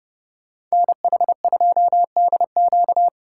Morse code, heard as a keyed tone, about 34 wpm, 700 Hz. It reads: NH2DQ
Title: Speed Racing - USA Call Signs 30wpm